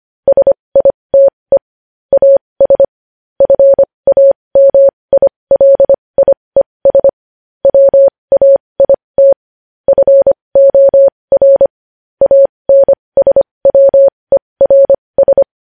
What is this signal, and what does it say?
Morse code, heard as a keyed tone, about 25 wpm, spaced as for 20 wpm, 550 Hz. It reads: SITE AS FAMILIES WAIT FOR ANSWERS
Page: News Headlines in Morse Code at 20 WPM